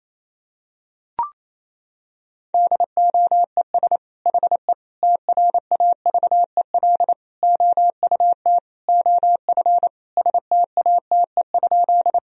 Morse code, heard as a keyed tone, about 28 wpm, 700 Hz. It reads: DOES HE TRAVEL OUT OF STATE?
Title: Sentences from Top 1000 Words - Review Part 2 - 28wpm